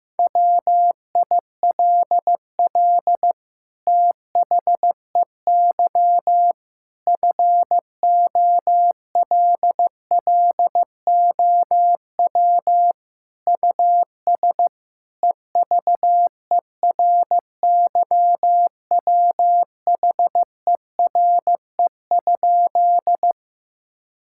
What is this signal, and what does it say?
Morse code, heard as a keyed tone, about 15 wpm, 700 Hz. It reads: WILL THEY FOLLOW US EVERYWHERE?
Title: Sentences from Top 1600 Words 15wpm